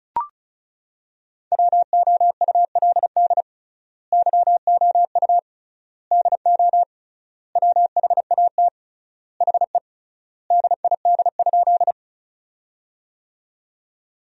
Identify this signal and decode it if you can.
Morse code, heard as a keyed tone, about 36 wpm, 700 Hz. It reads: WOULD YOU DO WHAT HE DID?
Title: 3x Word-Spacing - Sentences from Top 100 Words - 35wpm